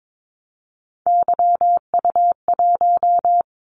Morse code, heard as a keyed tone, about 22 wpm, 700 Hz. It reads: YU1
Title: Callsign Prefixes for Top 25 DXCC - Repeated 3x - 22wpm